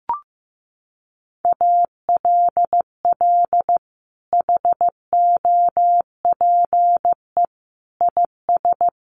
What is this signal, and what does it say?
Morse code, heard as a keyed tone, about 15 wpm, 700 Hz. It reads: ALL HOPE IS
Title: Sentences from Top 1400 Words 15wpm